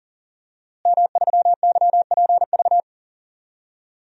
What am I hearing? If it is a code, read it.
Morse code, heard as a keyed tone, about 40 wpm, 700 Hz. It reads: M3YPV
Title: DXCC-100 Call Signs - 5 Characters - Repeated 2x - 40wpm